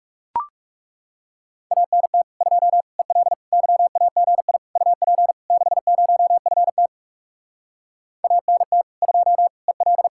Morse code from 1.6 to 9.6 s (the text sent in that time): ANT 2 EL YAGI UP 60FT ANT 2